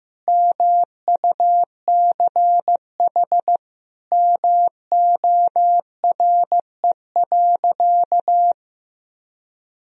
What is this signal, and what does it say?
Morse code, heard as a keyed tone, about 15 wpm, 700 Hz. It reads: MUCH MORE.
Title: Sentences from Top 1000 Words - Review Part 4 - 15wpm